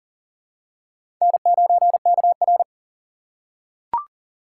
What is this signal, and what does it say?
Morse code, heard as a keyed tone, about 40 wpm, 700 Hz. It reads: N9KR
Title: K1USN SST QSO Practice - 40wpm